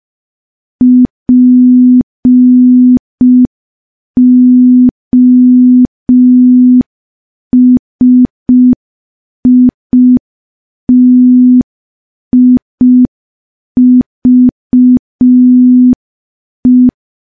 Morse code, heard as a keyed tone, about 5 wpm, 250 Hz. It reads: POSITIVE
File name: morse / morse-training-05WPM